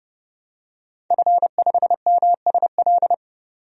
Morse code, heard as a keyed tone, about 30 wpm, 700 Hz. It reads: F5MSL
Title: DXCC-100 Call Signs - 5 Characters - 30wpm